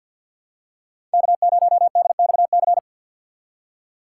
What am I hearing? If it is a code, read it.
Morse code, heard as a keyed tone, about 51 wpm, 700 Hz. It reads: K0DXC
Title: K1USN SST QSO Practice - 50wpm